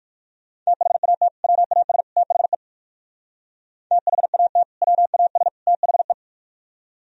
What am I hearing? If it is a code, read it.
Morse code, heard as a keyed tone, about 57 wpm, 700 Hz. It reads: THAT WAS THE THAT WAS THE